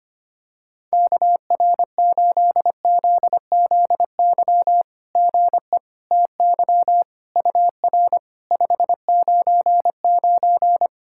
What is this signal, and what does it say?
Morse code, heard as a keyed tone, about 25 wpm, 700 Hz. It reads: KR8ZZY GE TY UR 599
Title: SOTA QSO Practice 25wpm